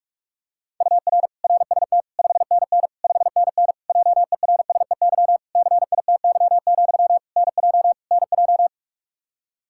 Morse code, heard as a keyed tone, about 47 wpm, 700 Hz. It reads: UR RST 5NN 5NN JERSEY CITY, NJ NJ
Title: Practice QSO - Exchange 1 - Beginning - 45wpm